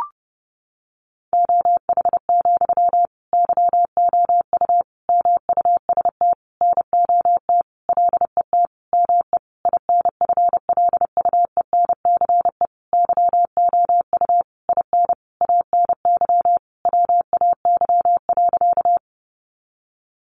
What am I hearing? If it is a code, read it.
Morse code, heard as a keyed tone, about 30 wpm, 700 Hz. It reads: OH, YOU MUST NOT LET ME INFLUENCE YOU IN ANY WAY.
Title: Sentences from Top 1100 Words 30wpm